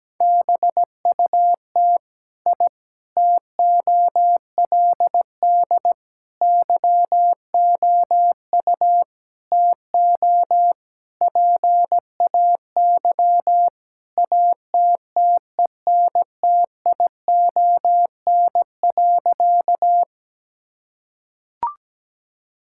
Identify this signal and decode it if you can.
Morse code, heard as a keyed tone, about 17 wpm, 700 Hz. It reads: BUT I TOLD YOU TO PAY ATTENTION.
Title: Sentences from Top 1000 Words - Review Part 6 - 17wpm